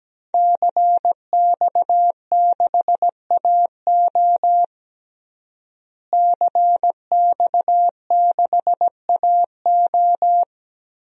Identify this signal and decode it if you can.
Morse code, heard as a keyed tone, about 17 wpm, 700 Hz. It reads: CX6AO CX6AO
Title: DXCC-100 Call Signs - 5 Characters - Repeated 3x - 17wpm